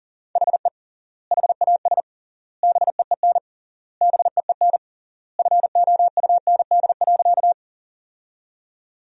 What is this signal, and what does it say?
Morse code, heard as a keyed tone, about 40 wpm, 700 Hz. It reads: HE HAS BEEN BEEN FOUND.
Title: 3x Word-Spacing - Sentences from Top 300 Words - 40wpm